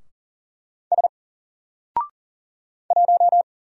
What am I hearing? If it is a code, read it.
Morse code, heard as a keyed tone, about 40 wpm, 700 Hz. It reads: S 1
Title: Single Character (Letter:Number:Punctuation:Prosign) 40wpm